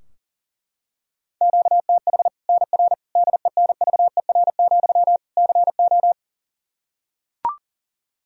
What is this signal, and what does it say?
Morse code, heard as a keyed tone, about 40 wpm, 700 Hz. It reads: QTH NR DENVER, CO
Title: K1USN SST QSO Practice 40wpm